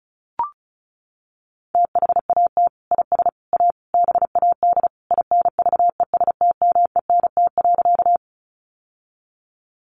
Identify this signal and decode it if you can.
Morse code, heard as a keyed tone, about 34 wpm, 700 Hz. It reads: THAT IS A BAD INVESTMENT.
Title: Sentences from Top 700 Words 35wpm